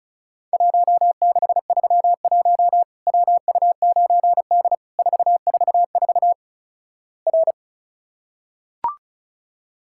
Morse code, heard as a keyed tone, about 36 wpm, 700 Hz. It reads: 1631 WU9D 444 R
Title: Worked All Europe DX Contest - QTC Exchange - 35wpm